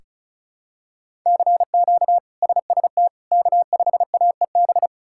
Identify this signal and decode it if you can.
Morse code, heard as a keyed tone, about 36 wpm, 700 Hz. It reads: CQ SST K5AEB
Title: K1USN SST QSO Practice - 35wpm